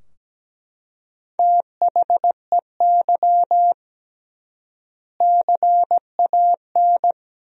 Morse code, heard as a keyed tone, about 17 wpm, 700 Hz. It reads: THEY CAN
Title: 3x Word-Spacing - Sentences from Top 400 Words - 17wpm